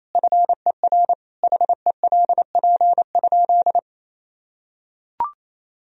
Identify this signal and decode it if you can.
Morse code, heard as a keyed tone, about 28 wpm, 700 Hz. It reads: FER HELP?
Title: Sentences from Top 200 Words and Abbreviations 28wpm